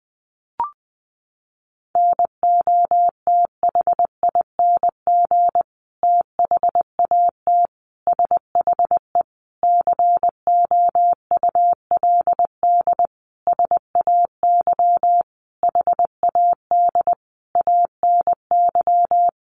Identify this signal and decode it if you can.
Morse code, heard as a keyed tone, about 20 wpm, 700 Hz. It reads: NOTHING THAT SHE COULD SAY HAD ANY
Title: Sentences from Top 1100 Words 20wpm